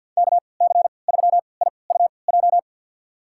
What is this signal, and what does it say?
Morse code, heard as a keyed tone, about 52 wpm, spaced as for 29 wpm, 700 Hz. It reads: KC3IUJ
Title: USA Call Signs - 6 Characters - 3x Character Spacing - 50wpm